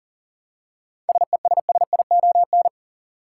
Morse code, heard as a keyed tone, about 40 wpm, 700 Hz. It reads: SESSION
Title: Seven Letter Words - 40wpm